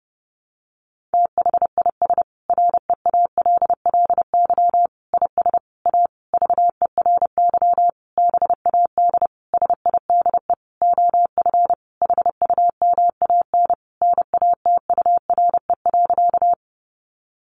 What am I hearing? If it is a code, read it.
Morse code, heard as a keyed tone, about 30 wpm, 700 Hz. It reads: THIS REALLY IS A VERY BAD SIDE OF HUMAN NATURE.
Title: Sentences from Top 700 Words 30wpm